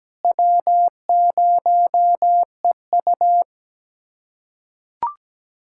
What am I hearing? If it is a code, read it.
Morse code, heard as a keyed tone, about 17 wpm, 700 Hz. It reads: W0EU
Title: USA Callsign DE Callsign 17wpm